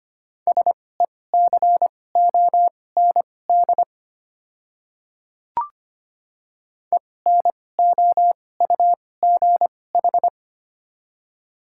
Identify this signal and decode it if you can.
Morse code, heard as a keyed tone, about 25 wpm, spaced as for 18 wpm, 700 Hz. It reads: SECOND ENOUGH
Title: 2x Character Spacing - Top 500 Words - 25wpm